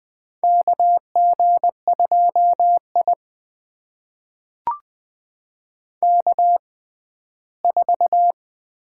Morse code, heard as a keyed tone, about 20 wpm, spaced as for 14 wpm, 700 Hz. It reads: KG2I K4
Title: USA Call Signs - 4 Characters - 6x Character Spacing - 20wpm